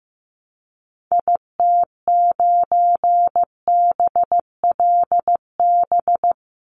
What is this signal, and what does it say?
Morse code, heard as a keyed tone, about 15 wpm, 700 Hz. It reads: IT9BLB
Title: Call Signs Repeated 3x - 15wpm